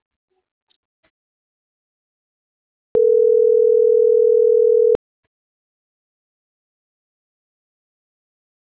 Ringback tone (North America), 440 + 480 Hz.